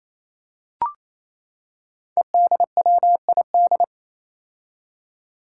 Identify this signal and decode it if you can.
Morse code, heard as a keyed tone, about 28 wpm, 700 Hz. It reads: EDWID
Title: Five Challenging Letter-Number Combinations 28wpm